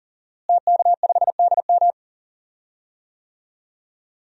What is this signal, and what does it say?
Morse code, heard as a keyed tone, about 38 wpm, 700 Hz. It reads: TK5DM
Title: DXCC-100 Call Signs - 5 Characters - Repeated 2x - 40wpm